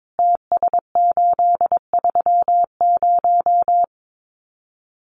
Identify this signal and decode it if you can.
Morse code, heard as a keyed tone, about 22 wpm, 700 Hz. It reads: TS830
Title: Common QSO Phrases 22wpm